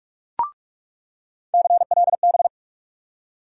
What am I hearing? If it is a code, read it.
Morse code, heard as a keyed tone, about 46 wpm, 700 Hz. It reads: CLB
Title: USA Call Sign Postfixes - 3 Letters 45wpm